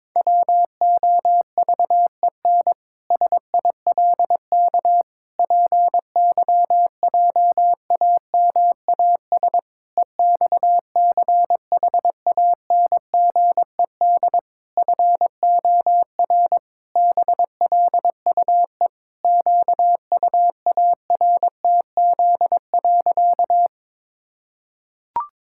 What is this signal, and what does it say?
Morse code, heard as a keyed tone, about 22 wpm, 700 Hz. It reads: WOVEN SILK PYJAMAS EXCHANGED FOR BLUE QUARTZ.
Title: Pangram Challenge v2 - 22wpm